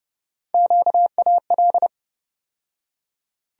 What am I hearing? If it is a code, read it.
Morse code, heard as a keyed tone, about 30 wpm, 700 Hz. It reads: QAL
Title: USA Call Sign Postfixes - 3 Letters 30wpm